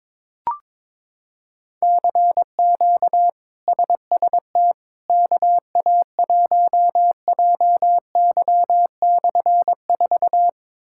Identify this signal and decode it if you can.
Morse code, heard as a keyed tone, about 22 wpm, 700 Hz. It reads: CQ SST KA1JY/4
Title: K1USN SST QSO Practice - 22wpm